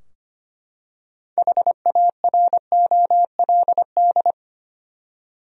Morse code, heard as a keyed tone, about 25 wpm, 700 Hz. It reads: HAROLD
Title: USA Common Names - 25wpm